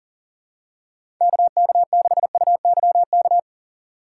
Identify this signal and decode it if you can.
Morse code, heard as a keyed tone, about 40 wpm, 700 Hz. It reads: KK6UYK
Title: USA Call Signs - 6 Characters - 4x Character Spacing - 40wpm